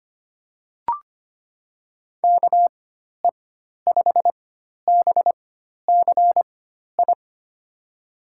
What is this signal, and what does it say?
Morse code, heard as a keyed tone, about 25 wpm, spaced as for 12 wpm, 700 Hz. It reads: KE5BCI
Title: USA Call Signs - 4x Character Spacing - 25wpm